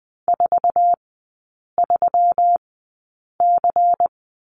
Morse code, heard as a keyed tone, about 20 wpm, spaced as for 8 wpm, 700 Hz. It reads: 43C
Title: 152 Lesson 25e - Letter p - Three Characters 20wpm